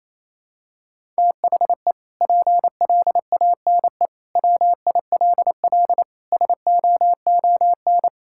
Morse code, heard as a keyed tone, about 28 wpm, 700 Hz. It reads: THE PLANE WILL SOON
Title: Sentences from Top 1000 Words - Review Part 6 - 28wpm